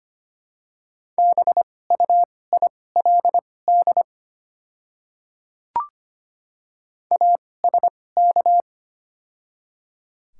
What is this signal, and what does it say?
Morse code, heard as a keyed tone, about 25 wpm, spaced as for 18 wpm, 700 Hz. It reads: BUILD ASK
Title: 2x Character Spacing - Top 200 Words - 25wpm